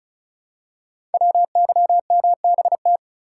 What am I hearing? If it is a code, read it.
Morse code, heard as a keyed tone, about 36 wpm, 700 Hz. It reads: WYMBT
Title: Code Groups - 5 Letters - 35wpm